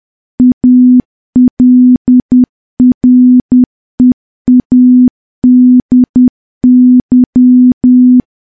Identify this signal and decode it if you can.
Morse code, heard as a keyed tone, about 10 wpm, 250 Hz. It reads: ALREADY
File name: morse / morse-training-10WPM